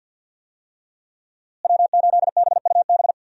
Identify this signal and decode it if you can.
Morse code, heard as a keyed tone, about 46 wpm, 700 Hz. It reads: W8BUB
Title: USA Call Signs - 5 Characters - Repeated 2x - 50wpm